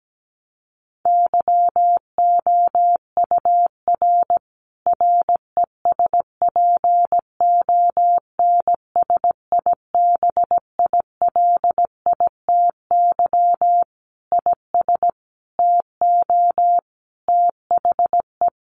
Morse code, heard as a keyed tone, about 17 wpm, 700 Hz. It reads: YOUR RESPONSIBILITY IS TO THE
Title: Sentences from Top 1200 Words 17wpm